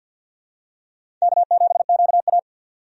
Morse code, heard as a keyed tone, about 50 wpm, 700 Hz. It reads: K7QA
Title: USA Call Signs - 4 Characters - 50wpm